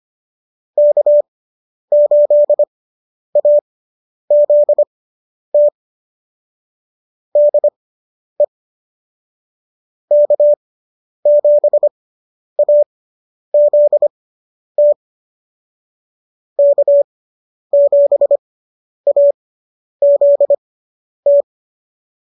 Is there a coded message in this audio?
Yes: Morse code from 0.8 to 21.4 s, heard as a keyed tone, about 25 wpm, spaced as for 10 wpm, 590 Hz. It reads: K8AZT DE K7AZT K7AZT